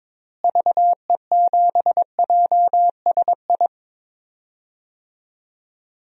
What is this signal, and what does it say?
Morse code, heard as a keyed tone, about 22 wpm, 700 Hz. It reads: VE7JSI
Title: DXCC-100 Call Signs - 6 Characters - Repeated 3x - 22wpm